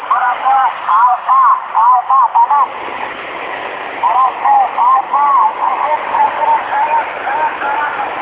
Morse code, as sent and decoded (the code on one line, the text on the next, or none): none